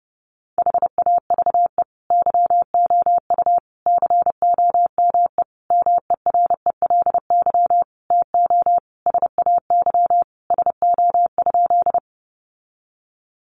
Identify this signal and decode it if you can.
Morse code, heard as a keyed tone, about 30 wpm, 700 Hz. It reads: HAVE YOU COME MERELY TO SAY SO?
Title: Sentences from Top 1400 Words 30wpm